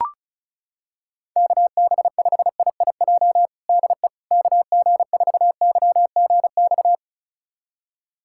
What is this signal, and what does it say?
Morse code, heard as a keyed tone, about 36 wpm, 700 Hz. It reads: KB5IIJ DE KG4YGX
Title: USA Callsign DE Callsign - 6 Characters - 35wpm